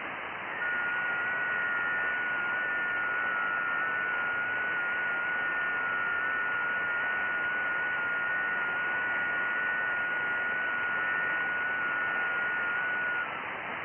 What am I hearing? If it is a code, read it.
FT8.
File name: IC-7100